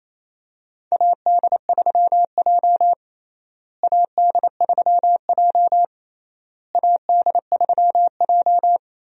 Morse code, heard as a keyed tone, about 27 wpm, 700 Hz. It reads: AD3J AD3J AD3J
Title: USA Call Signs - 4 Characters - Repeated 3x - 28wpm